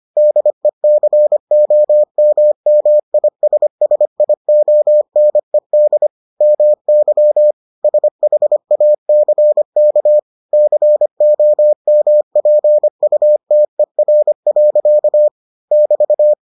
Morse code, heard as a keyed tone, about 25 wpm, 600 Hz. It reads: DECOMMISSIONED MY SHACK COMPUTER. <BT>